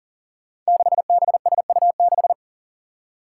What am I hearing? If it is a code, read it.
Morse code, heard as a keyed tone, about 38 wpm, 700 Hz. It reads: 6BSU6
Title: Five Challenging Letter-Number Combinations 40wpm